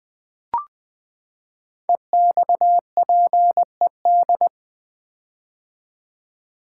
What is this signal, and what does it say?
Morse code, heard as a keyed tone, about 20 wpm, 700 Hz. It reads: EXPED
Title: Build A Word - 5 Letters At A Time - 10-Letter Words - 20wpm